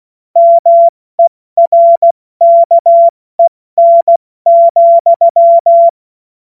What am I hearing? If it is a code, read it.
Morse code, heard as a keyed tone, about 16 wpm, 680 Hz. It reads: MERKEN,
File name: Morsekurs / Witze / Witz 02. August 2022 70 F80